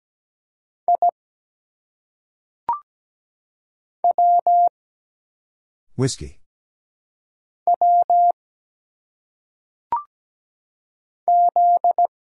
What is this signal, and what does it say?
Morse code, heard as a keyed tone, about 17 wpm, 700 Hz. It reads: I W W Z
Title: Single Letters - Phonetic Pronunciation 17wpm